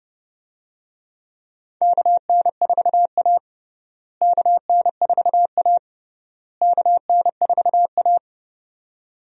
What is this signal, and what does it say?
Morse code, heard as a keyed tone, about 30 wpm, 700 Hz. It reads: KN4A KN4A KN4A
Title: USA Call Signs - 4 Characters - Repeated 3x - 30wpm